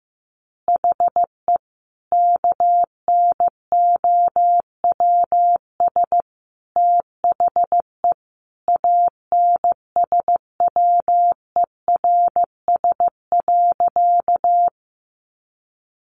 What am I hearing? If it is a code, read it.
Morse code, heard as a keyed tone, about 15 wpm, 700 Hz. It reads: HE KNOWS THE ANSWERS.
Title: Additional Easy Phrases 15wpm